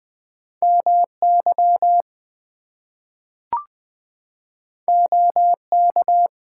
Morse code, heard as a keyed tone, about 20 wpm, 700 Hz. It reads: MY OK
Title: Two Letter Words - 20wpm